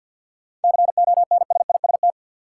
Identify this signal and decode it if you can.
Morse code, heard as a keyed tone, about 52 wpm, 700 Hz. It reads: CONSIST
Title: Sentences from Top 1400 Words - New Word Drill Alt - 50wpm